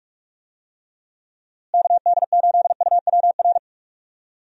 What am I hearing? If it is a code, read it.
Morse code, heard as a keyed tone, about 44 wpm, 700 Hz. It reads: KD8UWR
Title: USA Call Signs - 6 Characters - Repeated 2x - 45wpm